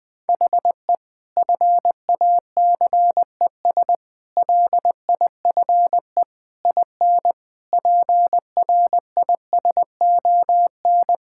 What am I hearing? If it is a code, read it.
Morse code, heard as a keyed tone, about 20 wpm, 700 Hz. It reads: HE FACES LIFE IN PRISON
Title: Sentences with 8-Word Phrases - 20wpm